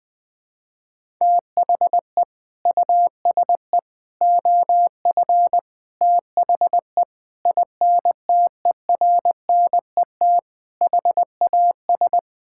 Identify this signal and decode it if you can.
Morse code, heard as a keyed tone, about 20 wpm, 700 Hz. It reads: THE USE OF THE INTERNET HAS
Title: Sentences with 4-Word Phrases - Spoken Beginning - 20wpm